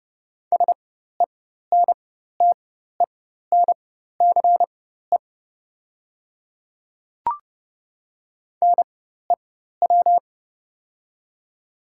Morse code, heard as a keyed tone, about 29 wpm, spaced as for 14 wpm, 700 Hz. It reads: SENTENCE NEW
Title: 4x Character Spacing - Top 200 Words - 30wpm